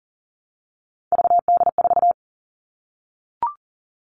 Morse code, heard as a keyed tone, about 40 wpm, 700 Hz. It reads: VD4
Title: Three Challenging Letter-Number Combinations 40wpm